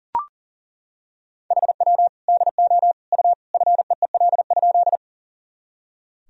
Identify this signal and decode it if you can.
Morse code, heard as a keyed tone, about 37 wpm, 700 Hz. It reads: HW DO U FEEL?
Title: Sentences from Top 400 Words and Abbreviations 40wpm